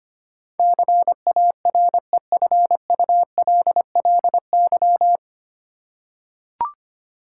Morse code, heard as a keyed tone, about 25 wpm, 700 Hz. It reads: CAREFULLY
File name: Nine Letter Words - 25wpm